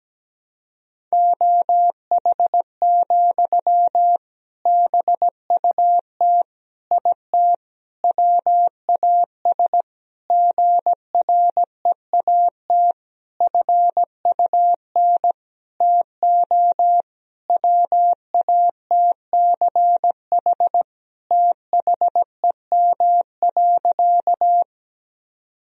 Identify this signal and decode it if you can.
Morse code, heard as a keyed tone, about 17 wpm, 700 Hz. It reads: OH, BUT IT WAS GREAT FUN TO WATCH THEM.
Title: Sentences from Top 1000 Words - Review Part 4 - 17wpm